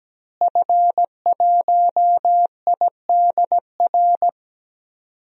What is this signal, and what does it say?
Morse code, heard as a keyed tone, about 17 wpm, 700 Hz. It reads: F1IDR
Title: DXCC-100 Call Signs - 5 Characters - Repeated 3x - 17wpm